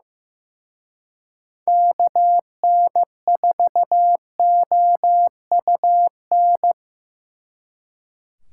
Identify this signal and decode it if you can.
Morse code, heard as a keyed tone, about 18 wpm, 700 Hz. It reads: KN4OUN
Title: Speed Racing - USA 2x3 Call Signs 15wpm